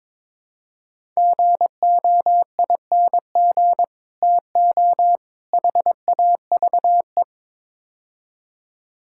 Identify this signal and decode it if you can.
Morse code, heard as a keyed tone, about 21 wpm, 700 Hz. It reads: GOING TO HAVE
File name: practice / 22wpm / Build A Phrase - 5-Word Phrases - 22wpm